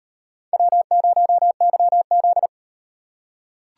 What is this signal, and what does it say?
Morse code, heard as a keyed tone, about 38 wpm, 700 Hz. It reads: W0YZ